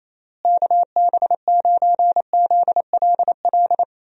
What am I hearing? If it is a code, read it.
Morse code, heard as a keyed tone, about 28 wpm, 700 Hz. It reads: KB9ZLL
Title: USA Call Signs - 6 Characters - Repeated 3x - 28wpm